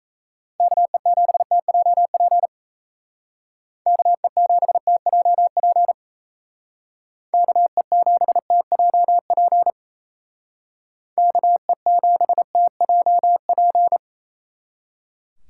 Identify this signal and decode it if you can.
Morse code, heard as a keyed tone, about 33 wpm, 700 Hz. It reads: KE7TJP KE7TJP KE7TJP KE7TJP
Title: Speed Racing - USA Call Signs 28wpm